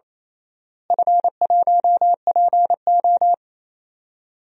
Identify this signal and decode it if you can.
Morse code, heard as a keyed tone, about 28 wpm, 700 Hz. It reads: F1PO
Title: DXCC-100 Call Signs - 4 Characters - Repeated 2x - 28wpm